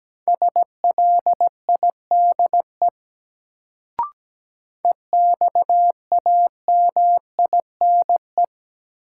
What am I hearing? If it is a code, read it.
Morse code, heard as a keyed tone, about 17 wpm, 700 Hz. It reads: SLIDE EXAMINE
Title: Sentences from Top 1600 Words - New Word Drill - 17wpm